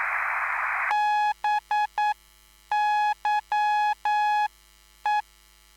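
Morse code, heard as a keyed tone, about 9 wpm, spaced as for 8 wpm, 840 Hz. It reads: BYE
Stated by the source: Live CW QSO between PA2BX and SP3JFK on the 14055 kHz QRS frequency, at a basic 8 wpm.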